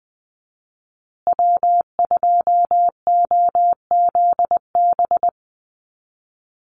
Morse code, heard as a keyed tone, about 20 wpm, 700 Hz. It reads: W2OZB
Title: Call Signs Repeated 2x - 20wpm